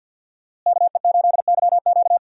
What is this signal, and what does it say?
Morse code, heard as a keyed tone, about 52 wpm, 700 Hz. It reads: KE8YQ
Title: USA Call Signs - 5 Characters - Repeated 2x - 50wpm